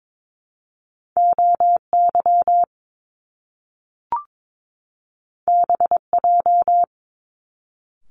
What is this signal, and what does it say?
Morse code, heard as a keyed tone, about 22 wpm, 700 Hz. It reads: OY BJ
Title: Top Two Letter Combinations in Words 22wpm